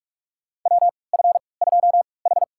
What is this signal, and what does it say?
Morse code, heard as a keyed tone, about 43 wpm, spaced as for 25 wpm, 690 Hz. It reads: WF2H
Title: USA Call Signs - 4 Characters - 3x Character Spacing - 45wpm